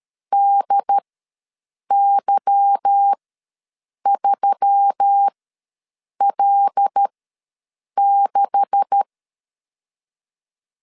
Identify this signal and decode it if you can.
Morse code, heard as a keyed tone, about 13 wpm, spaced as for 7 wpm, 800 Hz. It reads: DY3L6
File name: LEZIONI / 10 SIGNS / 10A SLASH ws 12wpm 3443